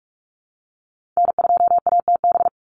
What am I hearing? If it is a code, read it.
Morse code, heard as a keyed tone, about 44 wpm, 700 Hz. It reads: N2ATB
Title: Call Signs - 45wpm